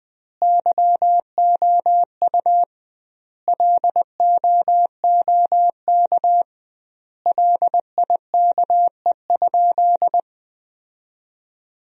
Morse code, heard as a keyed tone, about 20 wpm, 700 Hz. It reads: YOU LOOK LIKE?
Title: 2x Word-Spacing - Sentences from Top 100 Words - 20wpm